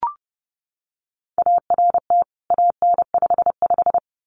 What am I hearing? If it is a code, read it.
Morse code, heard as a keyed tone, about 29 wpm, 700 Hz. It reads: ART AN55
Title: CWT Exchange 30wpm